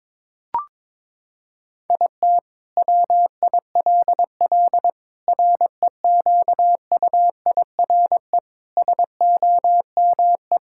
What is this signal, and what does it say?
Morse code, heard as a keyed tone, about 22 wpm, 700 Hz. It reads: IT WILL REQUIRE SOME
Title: Sentences from Top 1000 Words - Review Part 6 - 22wpm